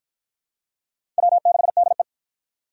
Morse code, heard as a keyed tone, about 51 wpm, 700 Hz. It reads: W6DE